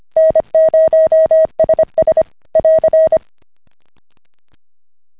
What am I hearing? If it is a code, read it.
Morse code, heard as a keyed tone, about 25 wpm, 640 Hz. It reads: N0SS <AR>